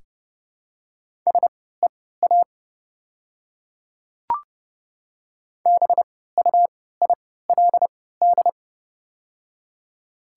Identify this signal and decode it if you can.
Morse code, heard as a keyed tone, about 30 wpm, spaced as for 17 wpm, 700 Hz. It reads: SEA BUILD
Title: 3x Character Spacing - Top 300 Words - 30wpm